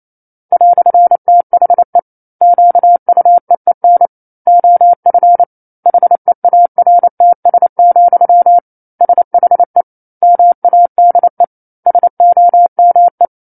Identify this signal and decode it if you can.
Morse code, heard as a keyed tone, about 28 wpm, 700 Hz. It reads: "THE QUEEN OF HEARTS, SHE MADE SOME
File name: 09-Chapter-09 28 WPM